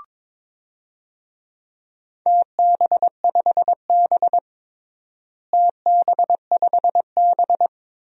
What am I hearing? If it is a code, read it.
Morse code, heard as a keyed tone, about 22 wpm, 700 Hz. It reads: TB5B TB5B
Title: DXCC-100 Call Signs - 4 Characters - Repeated 2x - 22wpm